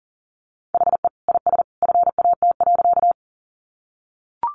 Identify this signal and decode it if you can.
Morse code, heard as a keyed tone, about 40 wpm, 700 Hz. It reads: HE IS FAT.
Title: Sentences from Top 600 Words 40wpm